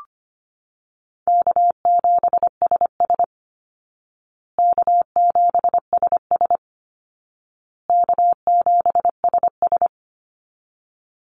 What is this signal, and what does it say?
Morse code, heard as a keyed tone, about 25 wpm, 700 Hz. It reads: K7SS K7SS K7SS
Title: Call Signs Repeated 2x - 25wpm